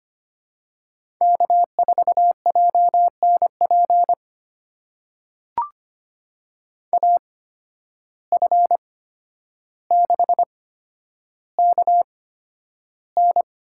Morse code, heard as a keyed tone, about 25 wpm, spaced as for 11 wpm, 700 Hz. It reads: K4JNP AF6KN
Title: USA Call Signs - 5 Characters - 8x Character Spacing - 25wpm